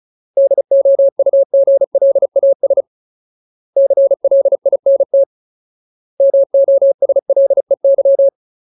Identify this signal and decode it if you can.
Morse code, heard as a keyed tone, about 34 wpm, 550 Hz. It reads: DOUGLAS CLINT MOSLEY
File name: Ultimate Word List / 35WPM Ultimate